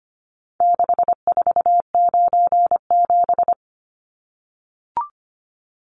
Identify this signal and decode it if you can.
Morse code, heard as a keyed tone, about 25 wpm, 700 Hz. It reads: 6497
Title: Numbers - 4-Digit 25wpm